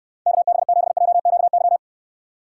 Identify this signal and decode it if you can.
Morse code, heard as a keyed tone, about 71 wpm, 700 Hz. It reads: KB7JQY